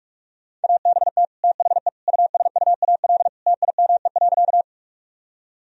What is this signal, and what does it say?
Morse code, heard as a keyed tone, about 46 wpm, 700 Hz. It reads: ABT THE USUAL TIME.
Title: Sentences from Top 400 Words and Abbreviations 45wpm